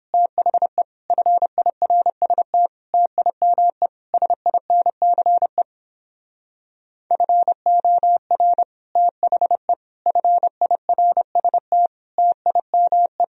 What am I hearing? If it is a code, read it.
Morse code, heard as a keyed tone, about 29 wpm, 700 Hz. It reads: THE FIRST TIME SINCE FOR THE FIRST TIME